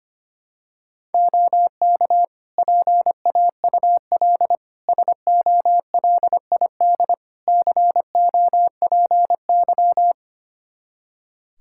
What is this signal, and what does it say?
Morse code, heard as a keyed tone, about 25 wpm, 700 Hz. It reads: OK PAUL SOLID COPY
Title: K1USN SST QSO Practice 25wpm